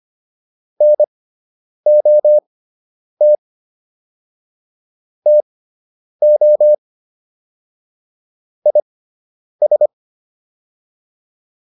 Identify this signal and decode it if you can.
Morse code, heard as a keyed tone, about 25 wpm, spaced as for 9 wpm, 600 Hz. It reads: NOT TO IS